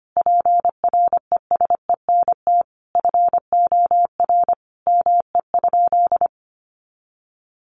Morse code, heard as a keyed tone, about 25 wpm, 700 Hz. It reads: PRESENT FOR ME?
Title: Sentences from Top 800 Words 25wpm